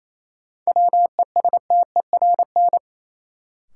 Morse code, heard as a keyed tone, about 28 wpm, 700 Hz. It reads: WESTERN
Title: Sentences from Top 1000 Words - Review Part 7 - New Word Drill - 28wpm